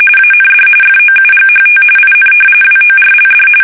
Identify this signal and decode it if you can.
RTTY, about 45.45 baud. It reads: AND FOUGHT IN WORLD
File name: fsk850 / battleshipnj